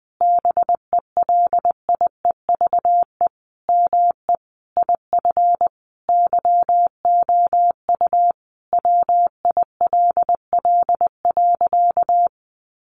Morse code, heard as a keyed tone, about 20 wpm, 700 Hz. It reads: BELIEVE ME IF YOU WILL.
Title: Sentences from Top 1000 Words 20wpm